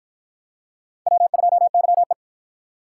Morse code, heard as a keyed tone, about 48 wpm, 700 Hz. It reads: W2YE